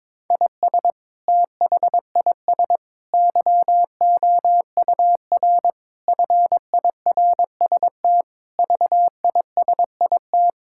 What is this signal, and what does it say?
Morse code, heard as a keyed tone, about 22 wpm, 700 Hz. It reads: IS THIS YOUR FIRST VISIT
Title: Sentences from Top 1000 Words - Review Part 6 - 22wpm